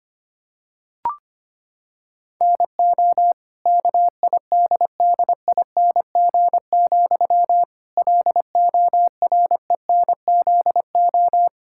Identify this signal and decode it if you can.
Morse code, heard as a keyed tone, about 25 wpm, 700 Hz. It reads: NO KIDDING, LORENZO
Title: Pangram Challenge v2 - 25wpm